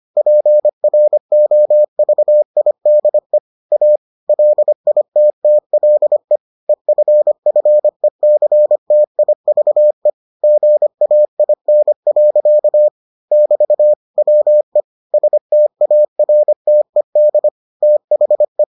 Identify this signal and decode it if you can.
Morse code, heard as a keyed tone, about 25 wpm, 600 Hz. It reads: PROVIDE A LITTLE EFFECTIVE GAIN. <BT> WE STARTED THE